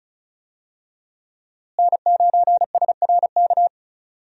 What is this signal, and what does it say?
Morse code, heard as a keyed tone, about 36 wpm, 700 Hz. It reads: N9SRK
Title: USA Call Signs - 5 Characters - Repeated 2x - 35wpm